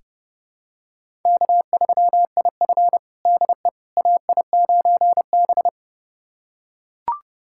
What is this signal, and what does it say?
Morse code, heard as a keyed tone, about 32 wpm, 700 Hz. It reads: K3IF DE AI9B
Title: USA Callsign DE Callsign - 4 Characters - 30wpm